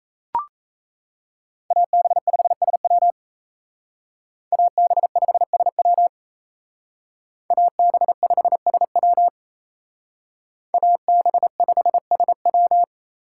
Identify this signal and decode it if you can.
Morse code, heard as a keyed tone, about 34 wpm, 700 Hz. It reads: AB5SW AB5SW AB5SW AB5SW
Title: Speed Racing - USA 2x2 Call Signs 28wpm